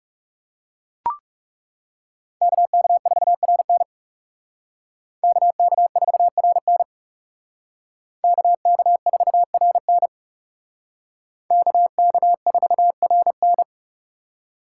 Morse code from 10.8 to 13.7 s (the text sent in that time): KK4RN